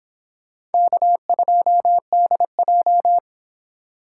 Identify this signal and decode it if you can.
Morse code, heard as a keyed tone, about 29 wpm, 700 Hz. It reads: K2DJ